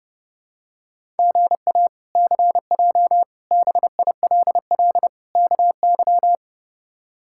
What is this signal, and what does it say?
Morse code, heard as a keyed tone, about 32 wpm, 700 Hz. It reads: GA CJ BILL KY
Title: K1USN SST QSO Practice - 30wpm